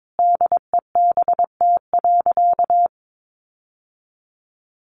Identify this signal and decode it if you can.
Morse code, heard as a keyed tone, about 22 wpm, 700 Hz. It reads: DEBT.
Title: Sentences from Top 1300 Words 22wpm